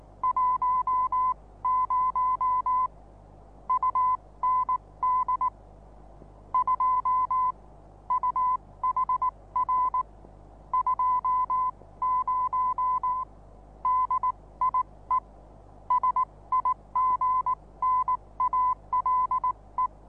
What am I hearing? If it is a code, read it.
Morse code, heard as a keyed tone, about 19 wpm, spaced as for 16 wpm, 1010 Hz. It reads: 10 UND 2 UHR 20 DIE SIGNALE
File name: repos / Morsekurs / DJ4UF Morsen26